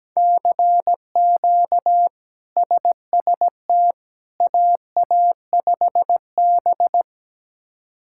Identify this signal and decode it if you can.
Morse code, heard as a keyed tone, about 17 wpm, 700 Hz. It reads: CQ SST AA5B
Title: K1USN SST QSO Practice - 17wpm